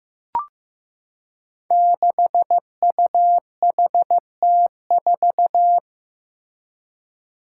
Morse code, heard as a keyed tone, about 15 wpm, 700 Hz. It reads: 6UHT4
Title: Five Challenging Letter-Number Combinations 15wpm